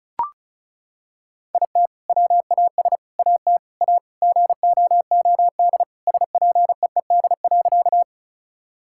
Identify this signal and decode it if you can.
Morse code, heard as a keyed tone, about 36 wpm, 700 Hz. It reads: IT WAS AT A GOOD SPEED.
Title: Sentences from Top 1000 Words - Review Part 5 - 35wpm